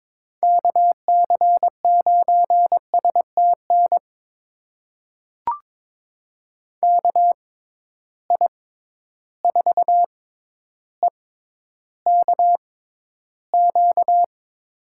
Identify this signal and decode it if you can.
Morse code, heard as a keyed tone, about 22 wpm, spaced as for 11 wpm, 700 Hz. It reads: KC9STN KI4EKQ
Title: USA Call Signs - 6 Characters - 6x Character Spacing - 22wpm